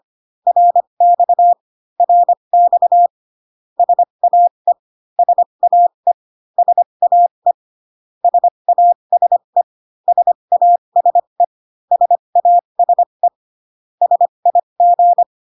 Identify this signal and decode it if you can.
Morse code, heard as a keyed tone, about 25 wpm, spaced as for 22 wpm, 700 Hz. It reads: RX RX SAE SAE SAE SASE SASE SASE SIG
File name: CW common abbreviations alphabetical repeated 3 times 25 18wpm